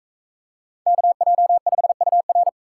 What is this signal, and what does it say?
Morse code, heard as a keyed tone, about 43 wpm, 700 Hz. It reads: KJ5UR